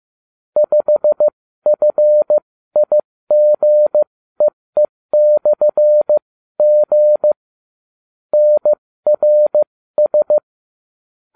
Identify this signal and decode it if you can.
Morse code, heard as a keyed tone, about 15 wpm, spaced as for 13 wpm, 600 Hz. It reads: 5FIGEE/G NRS